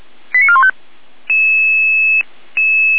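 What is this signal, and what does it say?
Five-tone selective call (ZVEI), heard as quick runs of five tones, tones of about 70 ms. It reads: 89415
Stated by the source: ZVEI five-tone call 88081 received by SDR.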